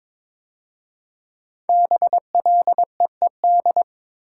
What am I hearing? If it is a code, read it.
Morse code, heard as a keyed tone, about 23 wpm, 700 Hz. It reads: BLEED
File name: 5 Letter Words - First 3 Letters Given - 22wpm